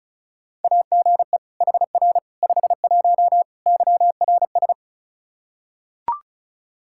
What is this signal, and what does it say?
Morse code, heard as a keyed tone, about 36 wpm, 700 Hz. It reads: AGE HR 51 YRS
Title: Practice QSO - Exchange 3 - More Info - 35wpm